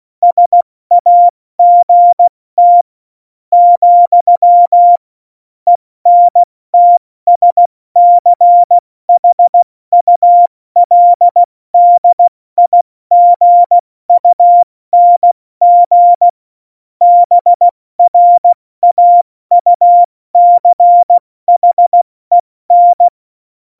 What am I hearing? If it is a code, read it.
Morse code, heard as a keyed tone, about 16 wpm, 700 Hz. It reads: SAGT , ENTSCHULDIGUNG BRAUCHEN
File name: Morsekurs / Witze / Witz 02. Oktober 2018, 70 F80